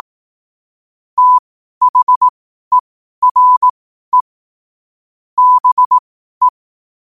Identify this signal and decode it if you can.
Morse code, heard as a keyed tone, about 18 wpm, spaced as for 12 wpm, 1000 Hz. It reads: THERE BE
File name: qotdfiles / 12wpm